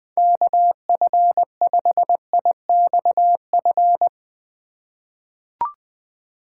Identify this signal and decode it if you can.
Morse code, heard as a keyed tone, about 20 wpm, 700 Hz. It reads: KF5IXF
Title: USA Call Signs - 6x Character Spacing - 20wpm